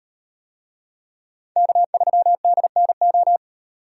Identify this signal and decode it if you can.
Morse code, heard as a keyed tone, about 38 wpm, 700 Hz. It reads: K3DNO